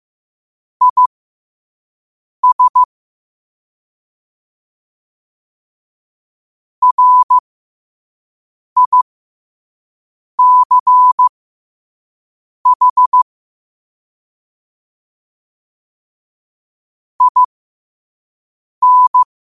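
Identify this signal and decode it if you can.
Morse code, heard as a keyed tone, about 15 wpm, spaced as for 5 wpm, 1000 Hz. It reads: IS RICH IN
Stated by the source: Morse code quote of the day at 5 wpm.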